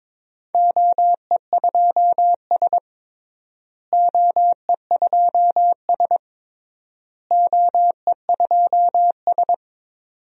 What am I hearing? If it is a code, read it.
Morse code, heard as a keyed tone, about 21 wpm, 700 Hz. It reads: OE2S OE2S OE2S
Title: DXCC-100 Call Signs - 4 Characters - Repeated 3x - 22wpm